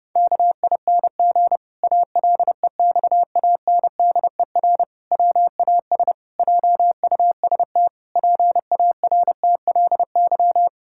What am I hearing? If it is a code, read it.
Morse code, heard as a keyed tone, about 30 wpm, 700 Hz. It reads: KING ALEXANDER WAS JUST PARTLY
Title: Pangram Challenge v2 - 30wpm